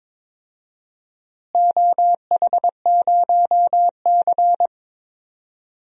Morse code, heard as a keyed tone, about 22 wpm, 700 Hz. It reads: OH0C
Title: DXCC-100 Call Signs - 4 Characters - Repeated 2x - 22wpm